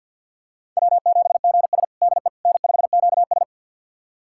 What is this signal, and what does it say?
Morse code, heard as a keyed tone, about 48 wpm, 700 Hz. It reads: W7GS DE N5QS
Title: USA Callsign DE Callsign - 4 Characters - 50wpm